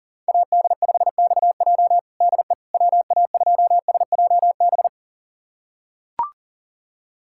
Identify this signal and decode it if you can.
Morse code, heard as a keyed tone, about 37 wpm, 700 Hz. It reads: AD5XJ DE WA2SJB
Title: USA Callsign DE Callsign 40wpm